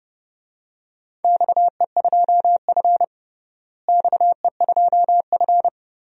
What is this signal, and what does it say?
Morse code, heard as a keyed tone, about 29 wpm, 700 Hz. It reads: XE2F XE2F
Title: DXCC-100 Call Signs - 4 Characters - Repeated 3x - 30wpm